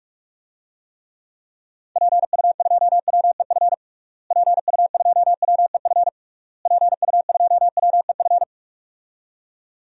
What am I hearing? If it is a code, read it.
Morse code, heard as a keyed tone, about 46 wpm, 700 Hz. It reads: PU2WEF PU2WEF PU2WEF
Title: DXCC-100 Call Signs - 6 Characters - Repeated 3x - 45wpm